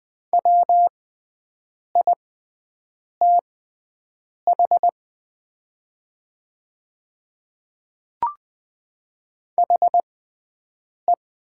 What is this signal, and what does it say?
Morse code, heard as a keyed tone, about 20 wpm, spaced as for 7 wpm, 700 Hz. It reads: WITH HE
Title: 6x Character Spacing - Top 100 Words - 20wpm